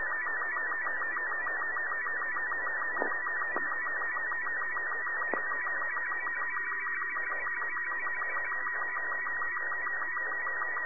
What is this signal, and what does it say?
SSTV (slow-scan television), Robot 36.
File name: sat / RADIOSCAF-B 24Dec2011